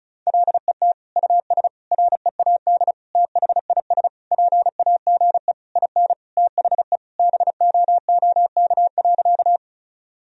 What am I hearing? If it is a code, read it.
Morse code, heard as a keyed tone, about 36 wpm, 700 Hz. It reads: LET US READ THIS PAGE IN THE BOOK.
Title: Sentences from Top 1000 Words - Review Part 1 - 35wpm